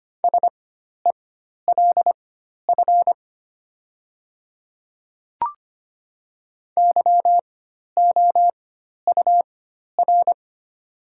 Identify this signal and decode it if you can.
Morse code, heard as a keyed tone, about 25 wpm, spaced as for 12 wpm, 700 Hz. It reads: SELF YOUR
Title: 4x Character Spacing - Top 200 Words - 25wpm